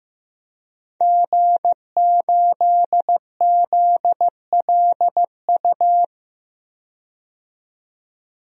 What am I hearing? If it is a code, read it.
Morse code, heard as a keyed tone, about 15 wpm, 700 Hz. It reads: G8ZLU
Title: DXCC-100 Call Signs - 5 Characters - 15wpm